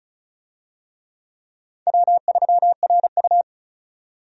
Morse code, heard as a keyed tone, about 36 wpm, 700 Hz. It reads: W3RU
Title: USA Call Signs - 4 Characters - 35wpm